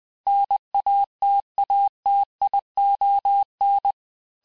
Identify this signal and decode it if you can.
Morse code, heard as a keyed tone, about 20 wpm, 800 Hz. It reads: NATATION
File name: cours-du-lundi files / cours-1584-teste-clair-vitesse-19mpm-puis-24mpm